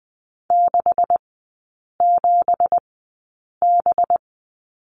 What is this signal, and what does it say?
Morse code, heard as a keyed tone, about 20 wpm, spaced as for 8 wpm, 700 Hz. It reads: 67B
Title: 184 Lesson 30e - Letter b - Three Characters 20wpm